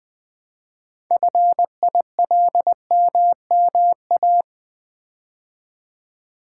Morse code, heard as a keyed tone, about 20 wpm, 700 Hz. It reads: FILMMA
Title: Build A Word - 3 Letters At A Time - 9-Letter Words - 20wpm